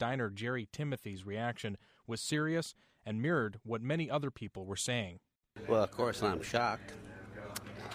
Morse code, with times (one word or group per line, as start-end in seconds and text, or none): none